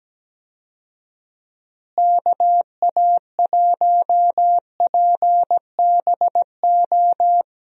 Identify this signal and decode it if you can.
Morse code, heard as a keyed tone, about 17 wpm, 700 Hz. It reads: KA1PBO
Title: USA Call Signs - 6 Characters - Repeated 2x - 17wpm